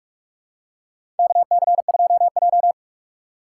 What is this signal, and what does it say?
Morse code, heard as a keyed tone, about 46 wpm, 700 Hz. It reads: KC2J
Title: USA Call Signs - 4 Characters - 2x Character Spacing - 45wpm